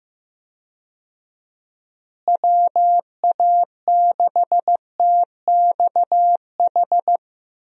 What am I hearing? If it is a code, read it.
Morse code, heard as a keyed tone, about 15 wpm, 700 Hz. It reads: WA6TXH
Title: USA Call Signs - 6 Characters - Repeated 2x - 15wpm